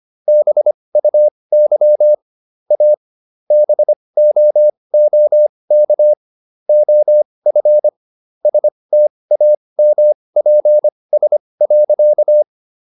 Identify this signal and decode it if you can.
Morse code, heard as a keyed tone, about 25 wpm, spaced as for 20 wpm, 600 Hz. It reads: BUY A BOOK OF STAMPS.